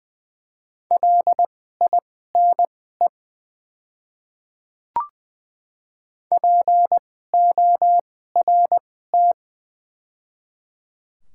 Morse code, heard as a keyed tone, about 20 wpm, spaced as for 14 wpm, 700 Hz. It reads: LINE PORT
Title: 2x Character Spacing - Top 200 Words - 20wpm